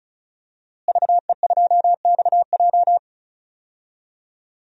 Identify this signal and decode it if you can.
Morse code, heard as a keyed tone, about 34 wpm, 700 Hz. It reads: VE2XJ
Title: DXCC-100 Call Signs - 5 Characters - Repeated 3x - 35wpm